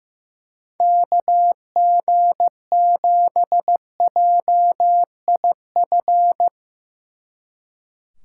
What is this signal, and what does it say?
Morse code, heard as a keyed tone, about 16 wpm, 700 Hz. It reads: KG7JIF
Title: Speed Racing - USA Call Signs 15wpm